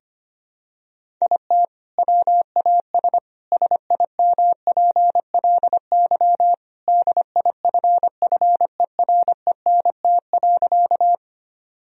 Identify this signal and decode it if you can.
Morse code, heard as a keyed tone, about 25 wpm, 700 Hz. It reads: IT WAS SIMPLY DIFFERENT.
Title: Sentences from Top 1000 Words - Review Part 4 - 25wpm